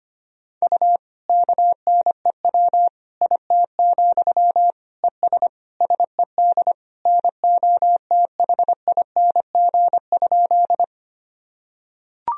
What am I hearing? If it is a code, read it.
Morse code, heard as a keyed tone, about 25 wpm, 700 Hz. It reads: U KNEW IT, ES SED NOTHING?
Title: Sentences from Top 500 Words and Abbreviations 25wpm